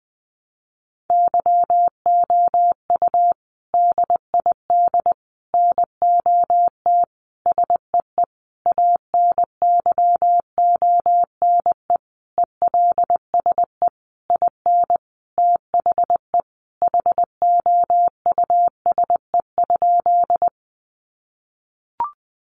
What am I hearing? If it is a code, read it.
Morse code, heard as a keyed tone, about 20 wpm, 700 Hz. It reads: YOU DID NOT SEE ANYONE ELSE IN THE HOUSE?
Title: Sentences from Top 900 Words 20wpm